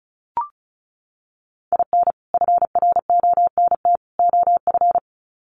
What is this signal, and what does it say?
Morse code, heard as a keyed tone, about 34 wpm, 700 Hz. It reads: IN FRONT OF
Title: Sets of 3 Words 35wpm